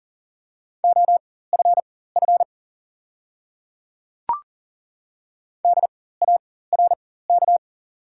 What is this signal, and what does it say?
Morse code, heard as a keyed tone, about 39 wpm, spaced as for 18 wpm, 700 Hz. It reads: OFF DARK
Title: 4x Character Spacing - Top 500 Words - 40wpm